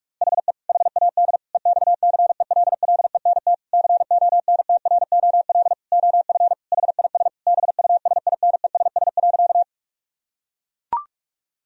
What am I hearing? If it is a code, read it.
Morse code, heard as a keyed tone, about 42 wpm, 700 Hz. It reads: HE HAD EXCELLENT CONTROL OF HIS BUSINESS.
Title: Sentences from Top 1000 Words - Review Part 3 - 45wpm